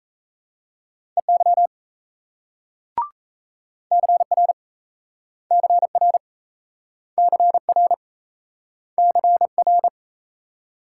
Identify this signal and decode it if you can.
Morse code, heard as a keyed tone, about 38 wpm, 700 Hz. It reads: EY CR CR CR CR